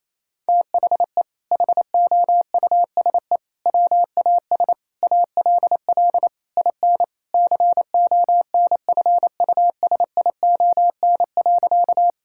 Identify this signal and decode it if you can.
Morse code, heard as a keyed tone, about 28 wpm, 700 Hz. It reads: THE HOUSE WAS ALL IN CONFUSION.
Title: Sentences from Top 1600 Words 28wpm